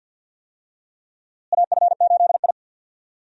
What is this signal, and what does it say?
Morse code, heard as a keyed tone, about 52 wpm, 700 Hz. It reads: AF8I
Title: USA Call Signs - 4 Characters - Repeated 3x - 50wpm